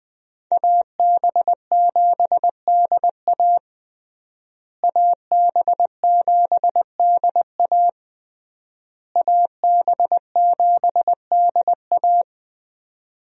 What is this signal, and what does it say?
Morse code, heard as a keyed tone, about 21 wpm, 700 Hz. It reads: AB7DA AB7DA AB7DA
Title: USA Call Signs - 5 Characters - Repeated 3x - 20wpm